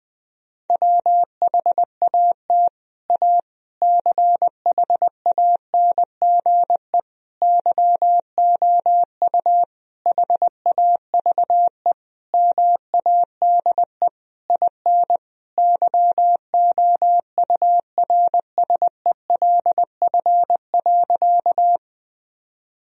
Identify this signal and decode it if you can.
Morse code, heard as a keyed tone, about 20 wpm, 700 Hz. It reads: WHAT A CHANGE YOU HAVE MADE IN YOURSELF.
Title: Sentences from Top 1000 Words - Review Part 1 - 20wpm